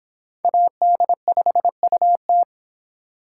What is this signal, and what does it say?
Morse code, heard as a keyed tone, about 26 wpm, 700 Hz. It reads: AD5UT